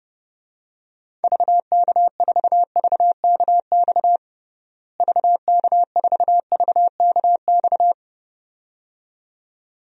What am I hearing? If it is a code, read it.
Morse code, heard as a keyed tone, about 29 wpm, 700 Hz. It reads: VK4VKX VK4VKX
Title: DXCC-100 Call Signs - 6 Characters - Repeated 2x - 30wpm